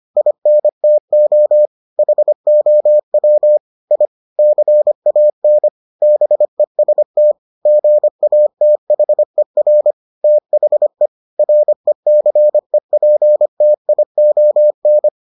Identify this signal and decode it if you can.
Morse code, heard as a keyed tone, about 25 wpm, 600 Hz. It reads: INTO HOW I CAN BEST GATHER THE RECEPTION